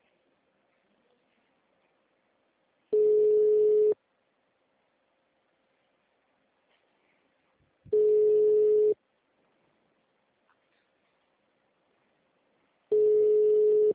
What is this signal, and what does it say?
Ringback tone (Europe, 425 Hz).